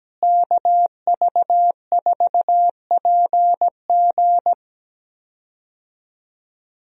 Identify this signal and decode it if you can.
Morse code, heard as a keyed tone, about 17 wpm, 700 Hz. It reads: KV4PG
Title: USA Call Signs - 5 Characters - Repeated 3x - 17wpm